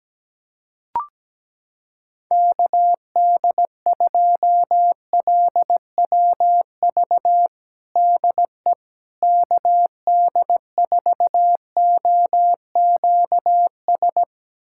Morse code, heard as a keyed tone, about 17 wpm, 700 Hz. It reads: KD2LWV DE KD4OQS
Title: USA Callsign DE Callsign - 6 Characters - 17wpm